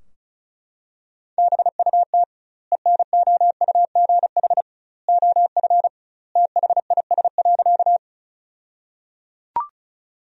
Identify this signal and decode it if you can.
Morse code, heard as a keyed tone, about 36 wpm, 700 Hz. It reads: BUT ENOUGH OF THIS.
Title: 2x Word-Spacing - Sentences from Top 400 Words - 35wpm